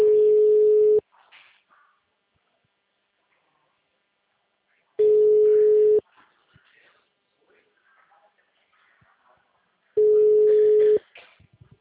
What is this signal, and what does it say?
Ringback tone (Europe, 425 Hz), with speech.